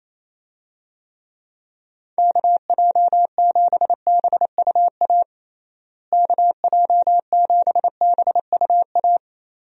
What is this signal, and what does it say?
Morse code, heard as a keyed tone, about 28 wpm, 700 Hz. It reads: KJ7BUA KJ7BUA
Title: USA Call Signs - 6 Characters - Repeated 3x - 28wpm